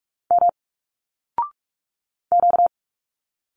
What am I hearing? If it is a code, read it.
Morse code, heard as a keyed tone, about 44 wpm, 700 Hz. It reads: M Q
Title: Single Letters 45wpm